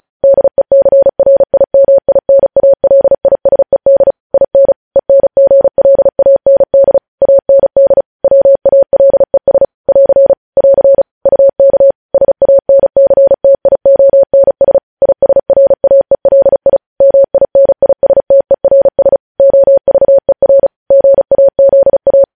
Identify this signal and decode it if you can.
Morse code, heard as a keyed tone, about 36 wpm, 550 Hz. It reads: DECRIMINALISED IN ENGLAND AND WALES <AR> <AR> UK SANCTIONS ISRAELI MINISTERS OVER GAZA
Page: news headlines in morse code at 35 wpm